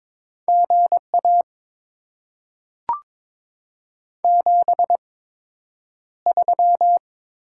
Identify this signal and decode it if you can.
Morse code, heard as a keyed tone, about 22 wpm, spaced as for 9 wpm, 700 Hz. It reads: GA 73
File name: Top 20 QSO Elements - 8x Character Spacing - 22wpm